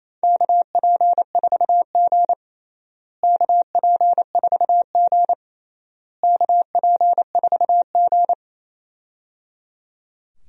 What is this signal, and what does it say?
Morse code, heard as a keyed tone, about 28 wpm, 700 Hz. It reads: KP4G KP4G KP4G
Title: DXCC-100 Call Signs - 4 Characters - Repeated 3x - 28wpm